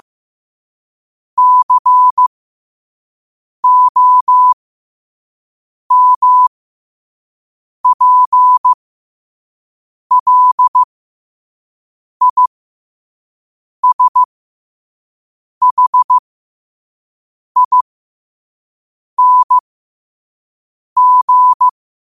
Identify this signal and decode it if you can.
Morse code, heard as a keyed tone, about 15 wpm, spaced as for 5 wpm, 1000 Hz. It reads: COMPLISHING
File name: qotdfiles / 5wpm